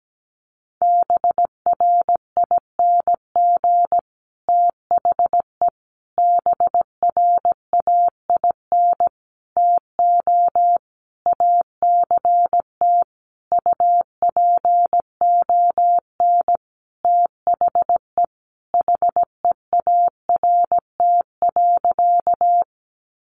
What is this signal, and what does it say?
Morse code, heard as a keyed tone, about 17 wpm, 700 Hz. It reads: BRING THE BRAIN TO ACT UPON THE HEART.
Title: Sentences from Top 1200 Words 17wpm